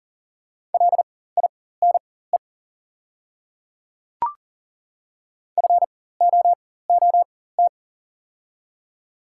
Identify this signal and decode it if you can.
Morse code, heard as a keyed tone, about 40 wpm, spaced as for 19 wpm, 700 Hz. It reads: LINE FOOT
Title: 4x Character Spacing - Top 500 Words - 40wpm